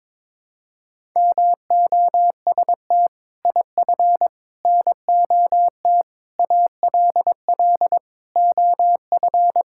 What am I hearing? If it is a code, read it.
Morse code, heard as a keyed tone, about 22 wpm, 700 Hz. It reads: MOST IF NOT ALL OF
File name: practice / 22wpm / Sets of 6 Words - 22wpm